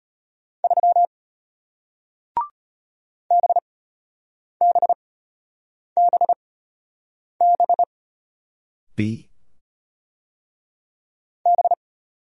Morse code, heard as a keyed tone, about 33 wpm, spaced as for 8 wpm, 700 Hz. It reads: 3 BBBB B